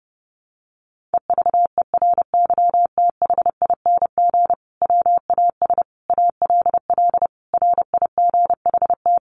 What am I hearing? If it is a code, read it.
Morse code, heard as a keyed tone, about 28 wpm, 700 Hz. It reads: EVERYTHING WAS ALL RIGHT
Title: Sentences from Top 700 Words 30wpm